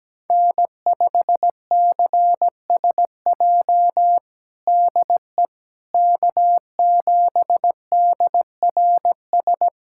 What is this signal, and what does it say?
Morse code, heard as a keyed tone, about 17 wpm, 700 Hz. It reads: N5CSJ DE K7DRS
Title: USA Callsign DE Callsign - 5 Characters - 17wpm